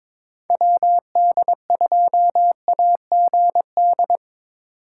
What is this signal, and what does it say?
Morse code, heard as a keyed tone, about 25 wpm, 700 Hz. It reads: WD2AGD
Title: Speed Racing - USA Call Signs 22wpm